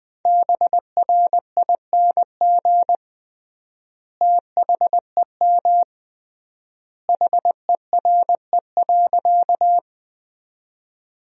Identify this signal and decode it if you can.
Morse code, heard as a keyed tone, about 21 wpm, 700 Hz. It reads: BRING THEM HERE.
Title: 3x Word-Spacing - Sentences from Top 500 Words - 20wpm